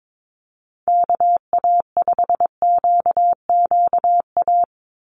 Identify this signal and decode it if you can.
Morse code, heard as a keyed tone, about 22 wpm, 700 Hz. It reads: KA5QQA
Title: NA SOTA Chaser Callsigns 22wpm